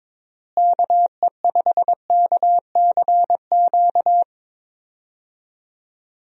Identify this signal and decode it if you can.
Morse code, heard as a keyed tone, about 22 wpm, 700 Hz. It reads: KE5KCQ
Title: USA Call Signs - 6 Characters - Repeated 2x - 22wpm